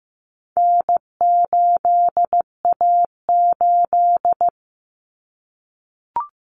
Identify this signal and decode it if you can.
Morse code, heard as a keyed tone, about 15 wpm, 700 Hz. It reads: N8A8
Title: Common Cut Numbers - 4-Digit 15wpm